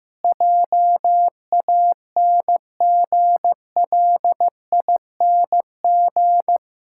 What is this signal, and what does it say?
Morse code, heard as a keyed tone, about 15 wpm, 700 Hz. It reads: JANGLING
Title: Pangram Challenge v2 - 15wpm